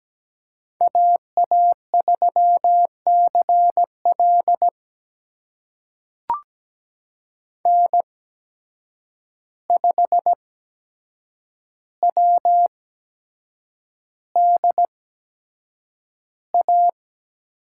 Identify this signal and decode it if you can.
Morse code, heard as a keyed tone, about 17 wpm, spaced as for 7 wpm, 700 Hz. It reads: AA3CL N5WDA
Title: USA Call Signs - 5 Characters - 8x Character Spacing - 17wpm